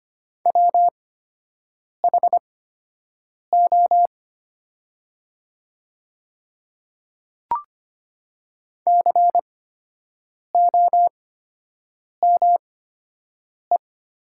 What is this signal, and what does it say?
Morse code, heard as a keyed tone, about 25 wpm, spaced as for 7 wpm, 700 Hz. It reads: WHO COME
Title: 8x Character Spacing - Top 100 Words - 25wpm